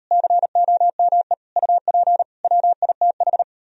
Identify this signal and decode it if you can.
Morse code, heard as a keyed tone, about 38 wpm, 700 Hz. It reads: COME UP WITH